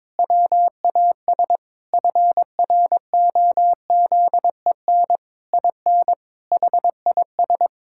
Morse code, heard as a keyed tone, about 22 wpm, 700 Hz. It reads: WAS FROZEN IN HIS
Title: Pangram Challenge v2 - 22wpm